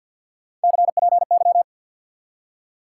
Morse code, heard as a keyed tone, about 46 wpm, 700 Hz. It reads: CPY
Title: Top 90 QSO Elements 50wpm